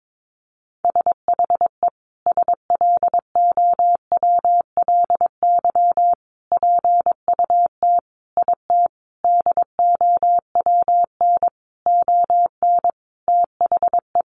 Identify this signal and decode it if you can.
Morse code, heard as a keyed tone, about 22 wpm, 700 Hz. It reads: SHE SLOWLY PUT IT DOWN ON THE
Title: Sentences from Top 1100 Words 22wpm